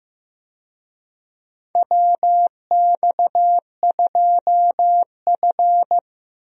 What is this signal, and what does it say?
Morse code, heard as a keyed tone, about 15 wpm, 700 Hz. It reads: WX2F
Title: USA Call Signs - 4 Characters - 15wpm